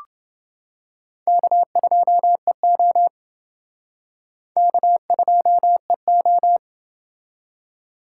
Morse code, heard as a keyed tone, about 28 wpm, 700 Hz. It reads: K2EO K2EO